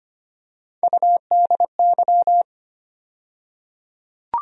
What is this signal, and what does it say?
Morse code, heard as a keyed tone, about 25 wpm, 700 Hz. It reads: UDY
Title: USA Call Sign Postfixes - 3 Letters 25wpm